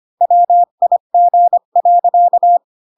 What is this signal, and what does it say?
Morse code, heard as a keyed tone, about 25 wpm, 700 Hz. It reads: WIG.
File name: Pinocchio / Pinocchio ch1 20 25 ik1hge